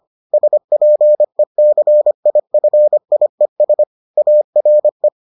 Morse code, heard as a keyed tone, about 25 wpm, 600 Hz. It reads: SPECIFIES ARE